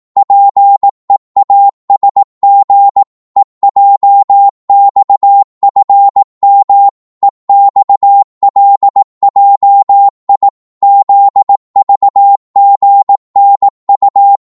Morse code, heard as a keyed tone, about 18 wpm, 800 Hz. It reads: PEASG EJXFM EXLJI ZVGNU
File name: Morse / ltr-18wpm